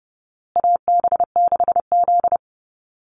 Morse code, heard as a keyed tone, about 30 wpm, 700 Hz. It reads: AB6Z
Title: Call Signs Repeated 2x - 30wpm